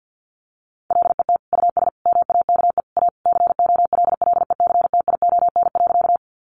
Morse code, heard as a keyed tone, about 50 wpm, 700 Hz. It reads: LET US MAKE A COLLECTION.
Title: Sentences from Top 1200 Words 50wpm